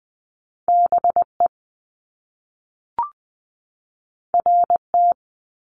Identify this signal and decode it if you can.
Morse code, heard as a keyed tone, about 20 wpm, 700 Hz. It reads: BE RT
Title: Top Two Letter Combinations in Words 20wpm